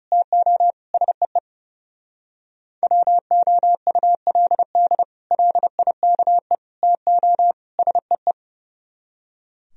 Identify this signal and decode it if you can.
Morse code, heard as a keyed tone, about 39 wpm, 700 Hz. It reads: TO SEE WOULD LIKE TO SEE